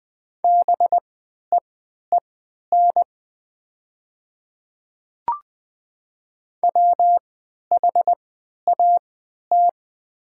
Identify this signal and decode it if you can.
Morse code, heard as a keyed tone, about 20 wpm, spaced as for 11 wpm, 700 Hz. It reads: BEEN WHAT
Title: 3x Character Spacing - Top 200 Words - 20wpm